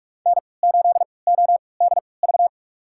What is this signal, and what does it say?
Morse code, heard as a keyed tone, about 41 wpm, spaced as for 25 wpm, 700 Hz. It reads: N8ODV
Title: USA Call Signs - 5 Characters - 3x Character Spacing - 45wpm